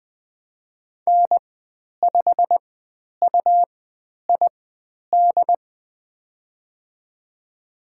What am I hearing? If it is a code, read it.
Morse code, heard as a keyed tone, about 20 wpm, spaced as for 10 wpm, 700 Hz. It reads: N5UID
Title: 109 Lesson 18g - Letter m - Call Signs 20wpm